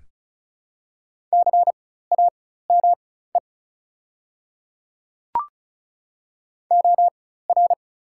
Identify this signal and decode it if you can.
Morse code, heard as a keyed tone, about 36 wpm, spaced as for 16 wpm, 690 Hz. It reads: CAME OR
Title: 4x Character Spacing - Top 500 Words - 35wpm